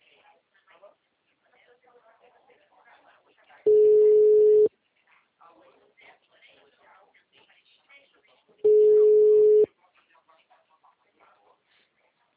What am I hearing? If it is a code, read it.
Ringback tone (Europe, 425 Hz), with speech.